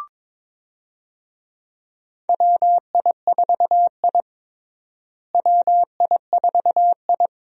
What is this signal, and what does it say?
Morse code, heard as a keyed tone, about 22 wpm, 700 Hz. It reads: WI4I WI4I
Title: USA Call Signs - 4 Characters - Repeated 3x - 22wpm